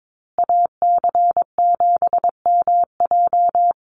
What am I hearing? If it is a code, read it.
Morse code, heard as a keyed tone, about 22 wpm, 700 Hz. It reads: AC7MJ
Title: USA Call Signs Repeated 2x 22wpm